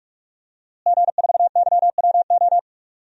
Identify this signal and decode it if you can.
Morse code, heard as a keyed tone, about 46 wpm, 700 Hz. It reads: G4YWO
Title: DXCC-100 Call Signs - 5 Characters - Repeated 3x - 45wpm